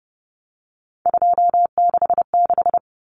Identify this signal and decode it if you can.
Morse code, heard as a keyed tone, about 30 wpm, 700 Hz. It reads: 266
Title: Common Cut Numbers - 3-Digit 30wpm